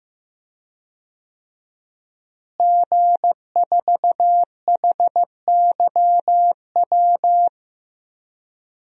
Morse code, heard as a keyed tone, about 15 wpm, 700 Hz. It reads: G4HYW
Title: DXCC-100 Call Signs - 5 Characters - Repeated 3x - 15wpm